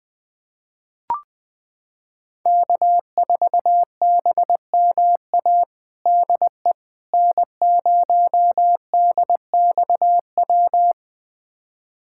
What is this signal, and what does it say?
Morse code, heard as a keyed tone, about 21 wpm, 700 Hz. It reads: K4BMA DE N0DXW
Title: USA Callsign DE Callsign - 5 Characters - 20wpm